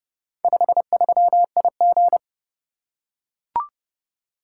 Morse code, heard as a keyed tone, about 28 wpm, 700 Hz. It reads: 53IG